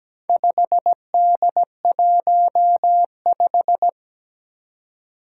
Morse code, heard as a keyed tone, about 17 wpm, 700 Hz. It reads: 5D15
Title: Four Challenging Letter-Number Combinations 17wpm